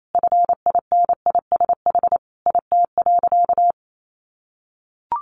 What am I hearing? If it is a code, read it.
Morse code, heard as a keyed tone, about 28 wpm, 700 Hz. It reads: FINISH IT.
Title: Sentences from Top 1200 Words 28wpm